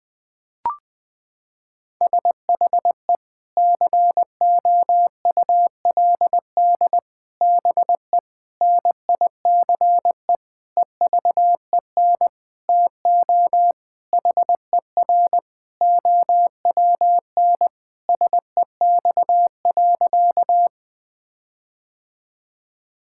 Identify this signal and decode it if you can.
Morse code, heard as a keyed tone, about 20 wpm, 700 Hz. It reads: SHE COULD BE NICE EVEN TO HER OWN SEX.
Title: Sentences from Top 1600 Words 20wpm